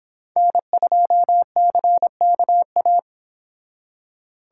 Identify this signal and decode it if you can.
Morse code, heard as a keyed tone, about 25 wpm, 700 Hz. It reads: N2CKA